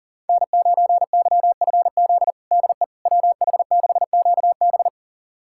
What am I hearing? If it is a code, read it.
Morse code, heard as a keyed tone, about 37 wpm, 700 Hz. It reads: N9YFZ DE WH6QB
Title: USA Callsign DE Callsign - 5 Characters - 40wpm